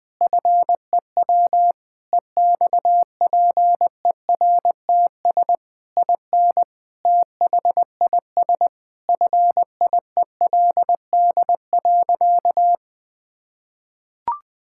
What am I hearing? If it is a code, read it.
Morse code, heard as a keyed tone, about 20 wpm, 700 Hz. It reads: FEW EXPERTS IN THIS FIELD.
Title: Sentences with 4-Word Phrases - 20wpm